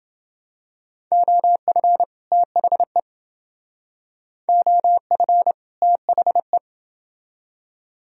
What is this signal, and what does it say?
Morse code, heard as a keyed tone, about 29 wpm, 700 Hz. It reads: OF THE OF THE